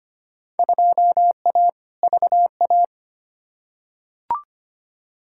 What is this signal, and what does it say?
Morse code, heard as a keyed tone, about 25 wpm, 700 Hz. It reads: 2A VA
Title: Field Day Exchanges - 25wpm